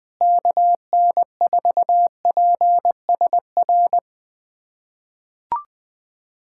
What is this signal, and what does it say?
Morse code, heard as a keyed tone, about 20 wpm, 700 Hz. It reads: KN4PSR
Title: USA Callsign DE Callsign - 6 Characters - 20wpm